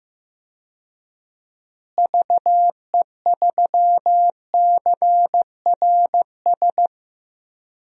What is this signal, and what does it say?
Morse code, heard as a keyed tone, about 15 wpm, 700 Hz. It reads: VE3CRS
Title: DXCC-100 Call Signs - 6 Characters - Repeated 2x - 15wpm